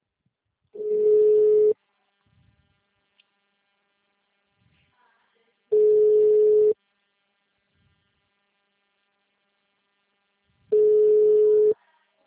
Ringback tone (Europe, 425 Hz).